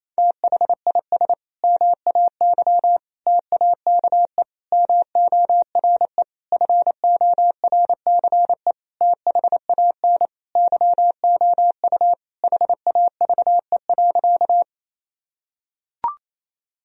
Morse code, heard as a keyed tone, about 28 wpm, 700 Hz. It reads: THIS MAY TAKE MORE FORCE THAN YOU HAVE.
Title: Sentences from Top 1000 Words - Review Part 3 - 28wpm